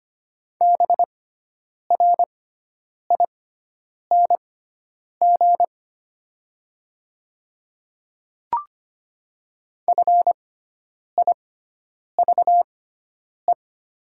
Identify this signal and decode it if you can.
Morse code, heard as a keyed tone, about 25 wpm, spaced as for 9 wpm, 700 Hz. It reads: BRING FIVE
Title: 6x Character Spacing - Top 500 Words - 25wpm